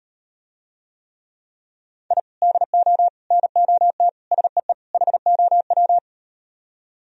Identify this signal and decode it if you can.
Morse code, heard as a keyed tone, about 38 wpm, 700 Hz. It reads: I DO NOT SEE HOW